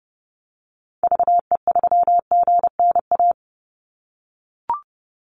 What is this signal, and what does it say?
Morse code, heard as a keyed tone, about 30 wpm, 700 Hz. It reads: VE3GNA
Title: Call Signs - 30wpm